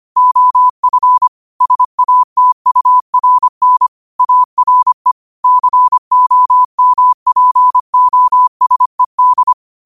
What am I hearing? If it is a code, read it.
Morse code, heard as a keyed tone, about 25 wpm, 1000 Hz. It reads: OF SATURN ARE COMPOSED
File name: 25wpm